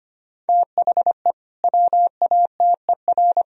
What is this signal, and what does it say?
Morse code, heard as a keyed tone, about 25 wpm, 700 Hz. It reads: THE WATER
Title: Sentences from Top 1000 Words - Review Part 4 - 25wpm